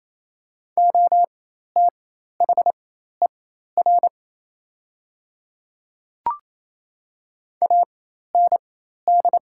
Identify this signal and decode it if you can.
Morse code, heard as a keyed tone, about 28 wpm, spaced as for 13 wpm, 700 Hz. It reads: OTHER AND
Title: 4x Character Spacing - Top 100 Words - 28wpm